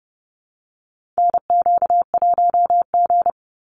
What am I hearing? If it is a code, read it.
Morse code, heard as a keyed tone, about 30 wpm, 700 Hz. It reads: NQ1G
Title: USA Call Signs 30wpm